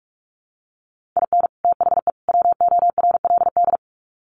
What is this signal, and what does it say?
Morse code, heard as a keyed tone, about 44 wpm, 700 Hz. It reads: IN THE WORLD
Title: Sets of 3 Words 45wpm